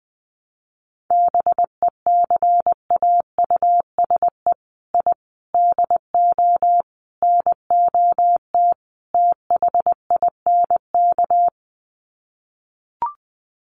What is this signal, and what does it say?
Morse code, heard as a keyed tone, about 19 wpm, 700 Hz. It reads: BECAUSE I DO NOT THINK
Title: Sets of 5 Words Encore 20wpm